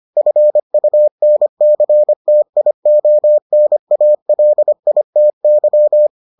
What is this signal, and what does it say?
Morse code, heard as a keyed tone, about 25 wpm, 600 Hz. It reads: FUNCTIONALITY